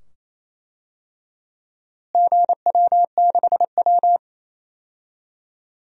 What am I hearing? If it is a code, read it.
Morse code, heard as a keyed tone, about 28 wpm, 700 Hz. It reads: GW6W
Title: DXCC-100 Call Signs - 4 Characters - Repeated 2x - 28wpm